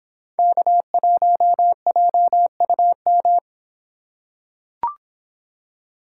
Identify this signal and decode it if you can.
Morse code, heard as a keyed tone, about 26 wpm, 700 Hz. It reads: K1JUM